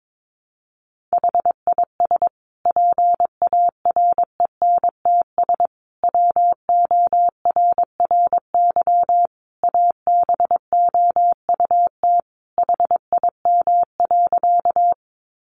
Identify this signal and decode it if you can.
Morse code, heard as a keyed tone, about 22 wpm, 700 Hz. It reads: HIS PARENTS WORRY ABOUT HIM.
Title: Easy Phrases 22wpm